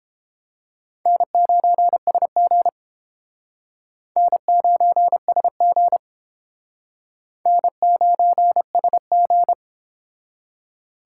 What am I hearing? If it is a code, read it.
Morse code, heard as a keyed tone, about 30 wpm, 700 Hz. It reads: N9SG N9SG N9SG